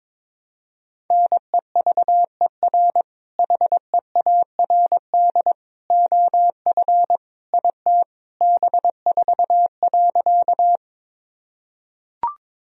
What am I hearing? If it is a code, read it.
Morse code, heard as a keyed tone, about 22 wpm, 700 Hz. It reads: NEVER HEARD OF IT B4.
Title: Sentences from Top 400 Words and Abbreviations 22wpm